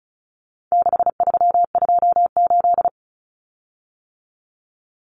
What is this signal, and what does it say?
Morse code, heard as a keyed tone, about 34 wpm, 700 Hz. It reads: 6328
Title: Common Cut Numbers - 4-Digit 35wpm